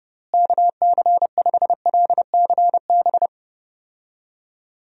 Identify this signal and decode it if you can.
Morse code, heard as a keyed tone, about 30 wpm, 700 Hz. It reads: KC5LCB
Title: USA Call Signs - 6 Characters - Repeated 2x - 30wpm